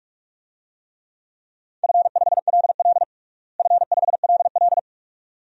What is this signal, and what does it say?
Morse code, heard as a keyed tone, about 42 wpm, 700 Hz. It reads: F5LL F5LL
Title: DXCC-100 Call Signs - 4 Characters - Repeated 2x - 45wpm